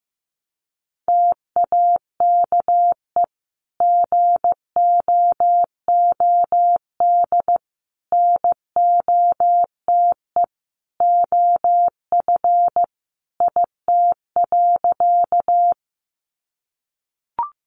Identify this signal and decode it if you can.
Morse code, heard as a keyed tone, about 15 wpm, 700 Hz. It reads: TAKE GOOD NOTE OF IT.
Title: Sentences from Top 500 Words 15wpm